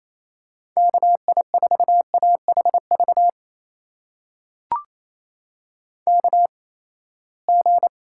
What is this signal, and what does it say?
Morse code, heard as a keyed tone, about 28 wpm, spaced as for 19 wpm, 700 Hz. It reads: KI4AHV KG
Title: USA Call Signs - 6 Characters - 8x Character Spacing - 28wpm